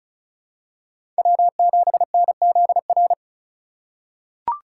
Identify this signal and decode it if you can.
Morse code, heard as a keyed tone, about 36 wpm, 700 Hz. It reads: W7NZR
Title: USA Call Signs - 5 Characters - 4x Character Spacing - 35wpm